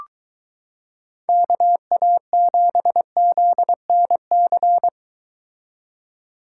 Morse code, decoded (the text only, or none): KA7ZNC